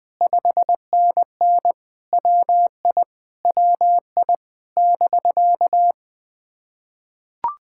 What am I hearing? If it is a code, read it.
Morse code, heard as a keyed tone, about 20 wpm, 700 Hz. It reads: 5NN WI WI <BK>
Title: POTA QSO Practice - 20wpm